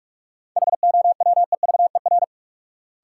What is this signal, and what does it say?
Morse code, heard as a keyed tone, about 46 wpm, 700 Hz. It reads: HOWEVER
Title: Sentences from Top 600 Words - New Word Drill - 45wpm